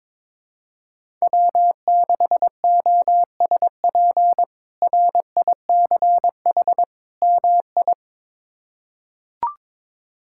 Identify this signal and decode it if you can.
Morse code, heard as a keyed tone, about 21 wpm, 700 Hz. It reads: W6OSP RICH MI
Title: K1USN SST QSO Practice - 22wpm